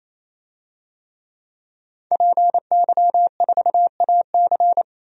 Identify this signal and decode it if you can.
Morse code, heard as a keyed tone, about 28 wpm, 700 Hz. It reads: PY4AC
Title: DXCC-100 Call Signs - 5 Characters - 28wpm